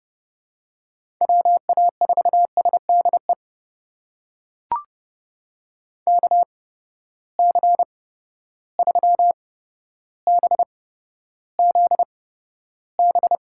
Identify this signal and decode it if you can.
Morse code, heard as a keyed tone, about 30 wpm, spaced as for 13 wpm, 700 Hz. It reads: WA4SDE KC3BZB
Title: USA Call Signs - 6 Characters - 8x Character Spacing - 30wpm